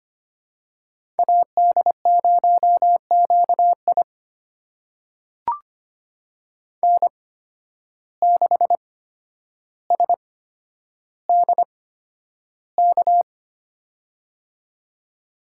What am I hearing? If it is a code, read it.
Morse code, heard as a keyed tone, about 25 wpm, spaced as for 11 wpm, 700 Hz. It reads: AD0QI N6SDK
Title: USA Call Signs - 5 Characters - 8x Character Spacing - 25wpm